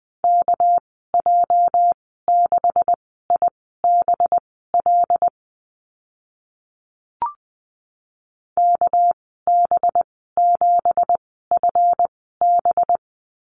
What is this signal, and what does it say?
Morse code, heard as a keyed tone, about 20 wpm, spaced as for 15 wpm, 700 Hz. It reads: KJ6IBL KB7FB
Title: 219 Lesson 35g - Number 0 - Call Signs 20wpm